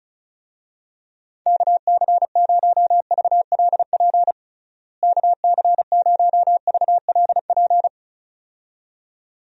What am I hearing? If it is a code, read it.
Morse code, heard as a keyed tone, about 36 wpm, 700 Hz. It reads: KC0VLP KC0VLP
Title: USA Call Signs - 6 Characters - Repeated 2x - 35wpm